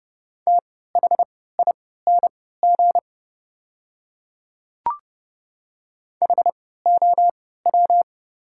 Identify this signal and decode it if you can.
Morse code, heard as a keyed tone, about 30 wpm, spaced as for 17 wpm, 700 Hz. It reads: THING HOW
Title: 3x Character Spacing - Top 200 Words - 30wpm